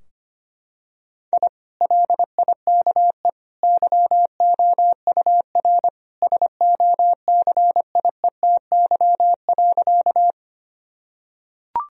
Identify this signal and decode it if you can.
Morse code, heard as a keyed tone, about 25 wpm, 700 Hz. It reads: I LIKE YOUR SOCIETY.
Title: Sentences from Top 1000 Words - Review Part 4 - 25wpm